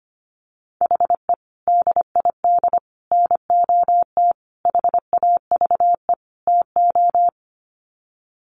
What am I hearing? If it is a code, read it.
Morse code, heard as a keyed tone, about 25 wpm, 700 Hz. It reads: HE DID NOT HAVE TO
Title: Sets of 5 Words 25wpm